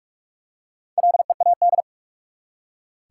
Morse code, heard as a keyed tone, about 46 wpm, 700 Hz. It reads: LEAD
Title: Four Letter Words - 45wpm